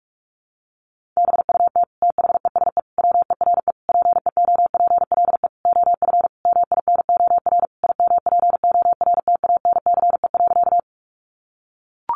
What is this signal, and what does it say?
Morse code, heard as a keyed tone, about 44 wpm, 700 Hz. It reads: BUT THESE WERE PEOPLE OF MINOR IMPORTANCE.
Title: Sentences from Top 1500 Words 45wpm